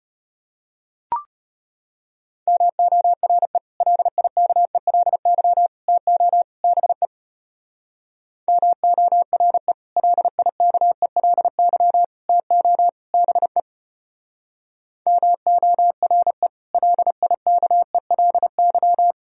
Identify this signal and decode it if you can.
Morse code, heard as a keyed tone, about 31 wpm, 700 Hz. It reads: MORE LIKELY TO BE MORE LIKELY TO BE MORE LIKELY
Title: Speed-Racing - Sets of 4 Words 25wpm